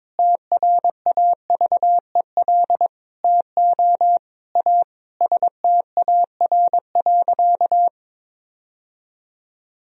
Morse code, heard as a keyed tone, about 22 wpm, 700 Hz. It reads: TRAVEL TO A STAR.
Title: Sentences from Top 1000 Words - Review Part 2 - 22wpm